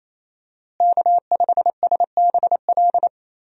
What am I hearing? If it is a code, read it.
Morse code, heard as a keyed tone, about 27 wpm, 700 Hz. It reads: K5SBL
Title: USA Call Signs - 5 Characters - Repeated 2x - 28wpm